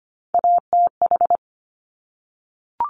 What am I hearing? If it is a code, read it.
Morse code, heard as a keyed tone, about 25 wpm, 700 Hz. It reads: ATH
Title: Top Three Letter Combinations in Words 25wpm